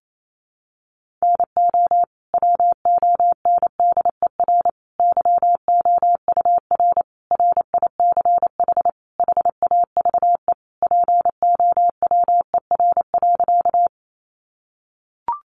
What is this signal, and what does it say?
Morse code, heard as a keyed tone, about 28 wpm, 700 Hz. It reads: NO WONDER YOUR RICH HAVE POWER.
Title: Sentences from Top 1100 Words 28wpm